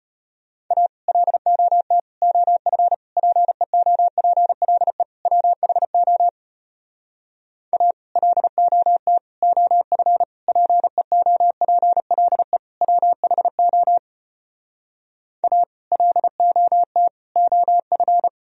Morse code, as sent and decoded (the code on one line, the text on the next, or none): .- / .-.. --- - / --- ..-. / .--. --- .--. .-.. / .-- .... --- / .- / .-.. --- - / --- ..-. / .--. --- .--. .-.. / .-- .... --- / .- / .-.. --- - / --- ..-.
A LOT OF PEOPLE WHO A LOT OF PEOPLE WHO A LOT OF